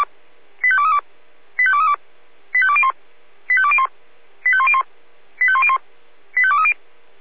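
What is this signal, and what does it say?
Five-tone selective call (ZVEI), heard as quick runs of five tones, tones of about 70 ms. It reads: 85232 85232 85201 85201 85101 85101 85230
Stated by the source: ZVEI five-tone selective call 85102, received via SDR.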